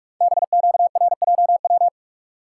Morse code, heard as a keyed tone, about 44 wpm, 700 Hz. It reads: BQRJW
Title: Code Groups - 5 Letters - 45wpm